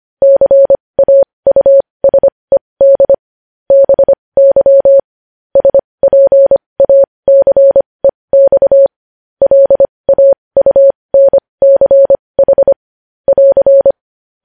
Morse code, heard as a keyed tone, about 25 wpm, spaced as for 20 wpm, 550 Hz. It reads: CAUSED BY SPACEX LAUNCH <AR>
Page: news-headlines-in-morse-code-at-20-wpm